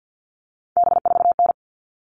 Morse code, heard as a keyed tone, about 48 wpm, 700 Hz. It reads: B4N
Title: All Cut Numbers - 3-Digit 50wpm